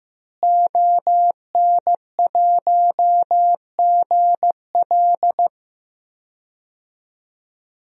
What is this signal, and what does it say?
Morse code, heard as a keyed tone, about 15 wpm, 700 Hz. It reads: ON1GL
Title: DXCC-100 Call Signs - 5 Characters - Repeated 2x - 15wpm